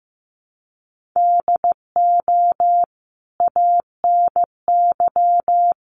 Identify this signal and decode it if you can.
Morse code, heard as a keyed tone, about 15 wpm, 700 Hz. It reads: DO ANY
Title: Sentences from Top 1100 Words 15wpm